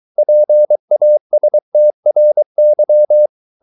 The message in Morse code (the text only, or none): PASTRY